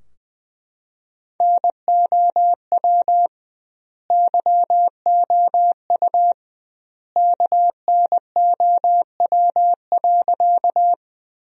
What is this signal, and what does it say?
Morse code, heard as a keyed tone, about 21 wpm, 700 Hz. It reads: NOW YOU KNOW.
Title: 2x Word-Spacing - Sentences from Top 100 Words - 20wpm